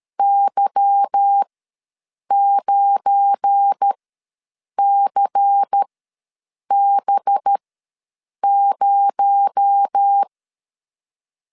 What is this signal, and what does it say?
Morse code, heard as a keyed tone, about 13 wpm, spaced as for 7 wpm, 800 Hz. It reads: Y9CB0
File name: LEZIONI / 9 28 ws 12wpm 3354